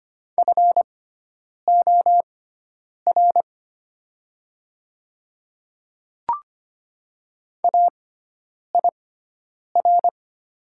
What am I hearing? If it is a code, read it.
Morse code, heard as a keyed tone, about 25 wpm, spaced as for 9 wpm, 700 Hz. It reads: FOR AIR
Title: 6x Character Spacing - Top 300 Words - 25wpm